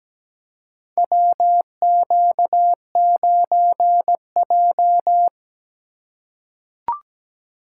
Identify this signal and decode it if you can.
Morse code, heard as a keyed tone, about 17 wpm, 700 Hz. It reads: WQ9J
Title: USA Call Signs - 4 Characters - 2x Character Spacing - 17wpm